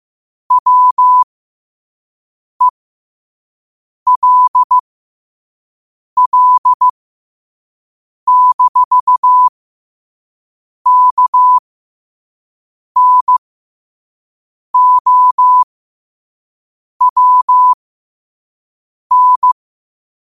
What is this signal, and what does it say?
Morse code, heard as a keyed tone, about 15 wpm, spaced as for 5 wpm, 1000 Hz. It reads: WELL-KNOWN
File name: qotdfiles / 5wpm